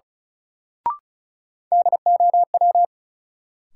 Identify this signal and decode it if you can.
Morse code, heard as a keyed tone, about 36 wpm, 700 Hz. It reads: DOW
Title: Sets of 5 Words - New Word Drill - 35wpm